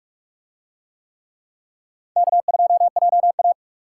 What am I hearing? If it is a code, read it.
Morse code, heard as a keyed tone, about 44 wpm, 700 Hz. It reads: K2JA